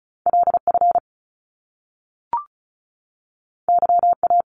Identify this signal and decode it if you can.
Morse code, heard as a keyed tone, about 35 wpm, 700 Hz. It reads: LF YA
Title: Top Two Letter Combinations in Words 35wpm